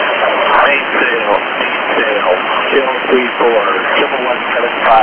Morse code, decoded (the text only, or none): none